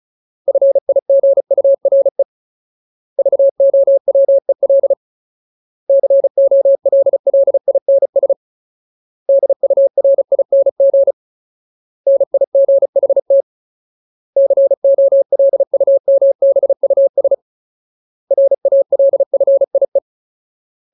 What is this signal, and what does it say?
Morse code, heard as a keyed tone, about 34 wpm, 550 Hz. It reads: FIGURE VOWEL COLLINS DURING NIGHT COLUMBUS RALFIE
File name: Ultimate Word List / 35WPM Ultimate